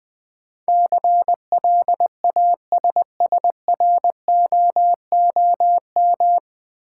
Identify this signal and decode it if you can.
Morse code, heard as a keyed tone, about 21 wpm, 700 Hz. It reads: CLASSROOM
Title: Sentences from Top 1500 Words - New Word Drill Alt - 20wpm